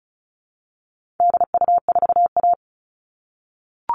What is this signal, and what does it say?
Morse code, heard as a keyed tone, about 34 wpm, 700 Hz. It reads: DU4A
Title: All Cut Numbers - 4-Digit 35wpm